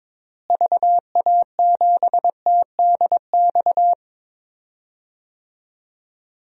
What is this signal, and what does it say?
Morse code, heard as a keyed tone, about 22 wpm, 700 Hz. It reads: VA7TDX
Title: DXCC-100 Call Signs - 6 Characters - Repeated 2x - 22wpm